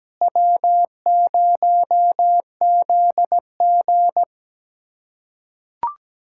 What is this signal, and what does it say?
Morse code, heard as a keyed tone, about 17 wpm, 700 Hz. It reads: W0ZG
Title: USA Callsign DE Callsign - 4 Characters - 17wpm